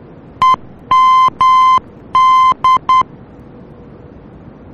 Morse code, heard as a keyed tone, about 10 wpm, 1040 Hz. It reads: EMD
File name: germany / mx / EMD-D